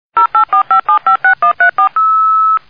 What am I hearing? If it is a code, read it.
DTMF, heard as telephone touch tones, tones of about 110 ms. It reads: 0845756134